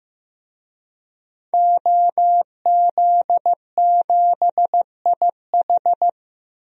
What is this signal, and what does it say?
Morse code, heard as a keyed tone, about 15 wpm, 700 Hz. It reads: OZ7IH
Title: DXCC-100 Call Signs - 5 Characters - Repeated 2x - 15wpm